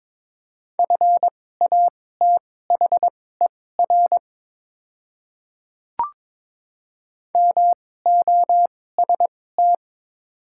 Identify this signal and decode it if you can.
Morse code, heard as a keyed tone, about 22 wpm, spaced as for 16 wpm, 700 Hz. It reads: FATHER MOST
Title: 2x Character Spacing - Top 300 Words - 22wpm